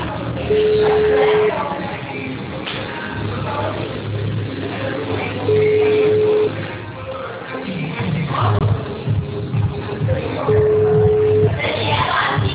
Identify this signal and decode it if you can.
Ringback tone (Europe, 425 Hz), with speech.
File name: out- -2015- - - .32777